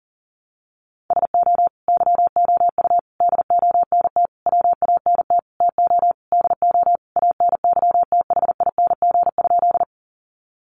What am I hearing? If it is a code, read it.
Morse code, heard as a keyed tone, about 40 wpm, 700 Hz. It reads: SO YOU DONT WANT TO DO ANYTHING?
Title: Sentences from Top 1000 Words 40wpm